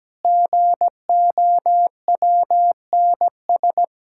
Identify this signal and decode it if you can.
Morse code, heard as a keyed tone, about 17 wpm, 700 Hz. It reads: GOWNS
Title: Pangram Challenge v2 - 17wpm